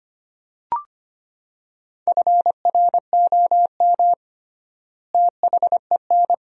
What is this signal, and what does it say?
Morse code, heard as a keyed tone, about 25 wpm, 700 Hz. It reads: FROM THEN
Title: 3x Word-Spacing - Sentences from Top 400 Words - 25wpm